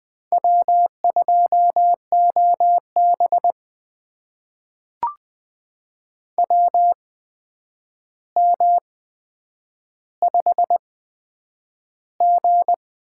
Morse code, heard as a keyed tone, about 20 wpm, spaced as for 9 wpm, 700 Hz. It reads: W2OB WM5G
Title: USA Call Signs - 4 Characters - 8x Character Spacing - 20wpm